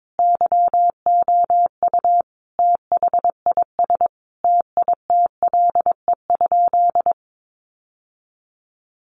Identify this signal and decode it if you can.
Morse code, heard as a keyed tone, about 22 wpm, 700 Hz. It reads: YOU THIS TITLE?
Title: Sentences from Top 1200 Words 22wpm